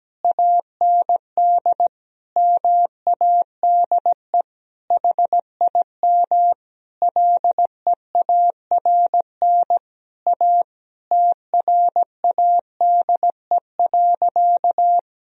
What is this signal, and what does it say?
Morse code, heard as a keyed tone, about 17 wpm, 700 Hz. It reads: AND MADE HIM LEARN A TRADE.
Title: Sentences from Top 1000 Words - Review Part 4 - 17wpm